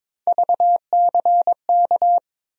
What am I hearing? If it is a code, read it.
Morse code, heard as a keyed tone, about 21 wpm, 700 Hz. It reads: VCK
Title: USA Call Sign Postfixes - 3 Letters 22wpm